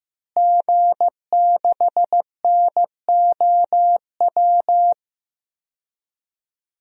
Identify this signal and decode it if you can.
Morse code, heard as a keyed tone, about 15 wpm, 700 Hz. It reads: G6NOW
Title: DXCC-100 Call Signs - 5 Characters - Repeated 3x - 15wpm